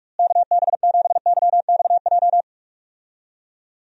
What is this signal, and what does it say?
Morse code, heard as a keyed tone, about 46 wpm, 700 Hz. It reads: KB7YXJ
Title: USA Call Signs - 6 Characters - Repeated 2x - 45wpm